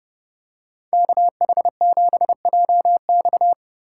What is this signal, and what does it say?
Morse code, heard as a keyed tone, about 31 wpm, 700 Hz. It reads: KH7JX